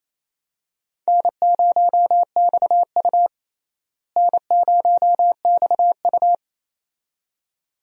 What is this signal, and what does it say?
Morse code, heard as a keyed tone, about 28 wpm, 700 Hz. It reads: N0XU N0XU
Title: USA Call Signs - 4 Characters - Repeated 2x - 28wpm